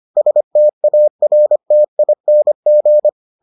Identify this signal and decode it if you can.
Morse code, heard as a keyed tone, about 25 wpm, 600 Hz. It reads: STARTING